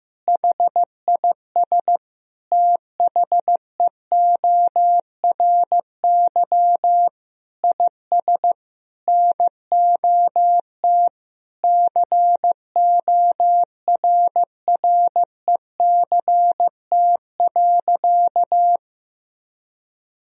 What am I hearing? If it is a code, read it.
Morse code, heard as a keyed tone, about 16 wpm, 700 Hz. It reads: HIS THEORY IS NOT CORRECT.
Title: Sentences from Top 1000 Words - Review Part 3 - 15wpm